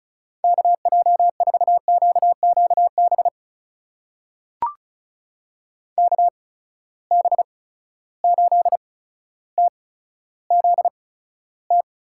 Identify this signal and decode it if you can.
Morse code, heard as a keyed tone, about 34 wpm, spaced as for 15 wpm, 700 Hz. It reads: KJ4QQB KB8TZT
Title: USA Call Signs - 6 Characters - 8x Character Spacing - 35wpm